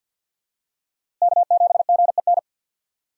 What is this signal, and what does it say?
Morse code, heard as a keyed tone, about 52 wpm, 700 Hz. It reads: K7GEN
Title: USA Call Signs - 5 Characters - Repeated 3x - 50wpm